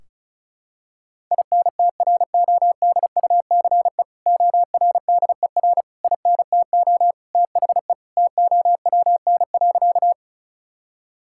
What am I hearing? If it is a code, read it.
Morse code, heard as a keyed tone, about 36 wpm, 700 Hz. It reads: INTRODUCE ORDER INTO THE TOWN.
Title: Sentences from Top 1600 Words 35wpm